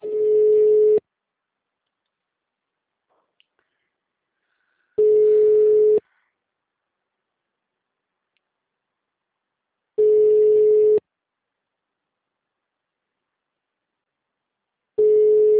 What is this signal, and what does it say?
Ringback tone (Europe, 425 Hz).